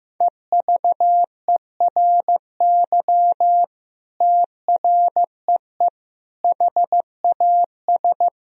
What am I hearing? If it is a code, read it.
Morse code, heard as a keyed tone, about 16 wpm, 700 Hz. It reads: EVERY TREE HAS
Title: Sentences from Top 1000 Words - Review Part 1 - 15wpm